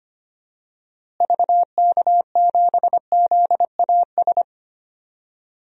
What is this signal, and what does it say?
Morse code, heard as a keyed tone, about 25 wpm, 700 Hz. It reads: VK7ZAS
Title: DXCC-100 Call Signs - 6 Characters - Repeated 2x - 25wpm